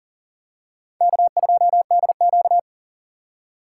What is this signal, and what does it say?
Morse code, heard as a keyed tone, about 40 wpm, 700 Hz. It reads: K2DQ
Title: USA Call Signs - 2x Character Spacing - 40wpm